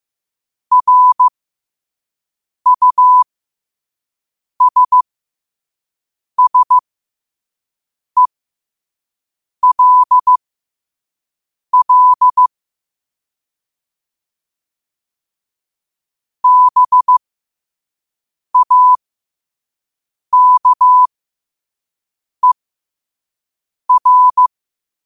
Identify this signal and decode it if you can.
Morse code, heard as a keyed tone, about 15 wpm, spaced as for 5 wpm, 1000 Hz. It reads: RUSSELL BAKER
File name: qotdfiles / 5wpm